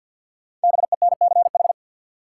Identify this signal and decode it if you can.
Morse code, heard as a keyed tone, about 53 wpm, 700 Hz. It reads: BENCH
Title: Sentences from Top 1400 Words - New Word Drill Alt - 50wpm